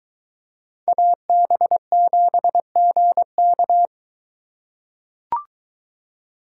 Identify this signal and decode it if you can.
Morse code, heard as a keyed tone, about 21 wpm, 700 Hz. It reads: AB7GK